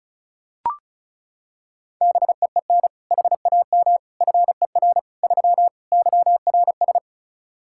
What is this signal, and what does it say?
Morse code, heard as a keyed tone, about 36 wpm, 700 Hz. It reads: BEEN HAM FER 3 YRS
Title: Practice QSO - Exchange 3 - More Info - 35wpm